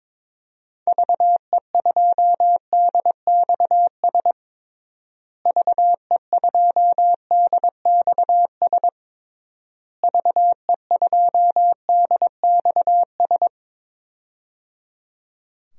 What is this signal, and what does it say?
Morse code, heard as a keyed tone, about 22 wpm, 700 Hz. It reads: VE2DXS VE2DXS VE2DXS
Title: DXCC-100 Call Signs - 6 Characters - Repeated 3x - 22wpm